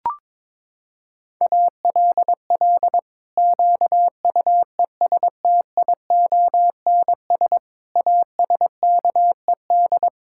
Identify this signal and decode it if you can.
Morse code, heard as a keyed tone, about 22 wpm, 700 Hz. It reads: ALL QUESTIONS ASKED
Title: Pangram Challenge v2 - 22wpm